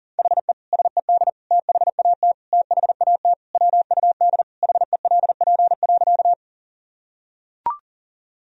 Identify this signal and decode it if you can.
Morse code, heard as a keyed tone, about 40 wpm, 700 Hz. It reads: HE SED THAT THAT WUD HELP.
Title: Sentences from Top 200 Words and Abbreviations 40wpm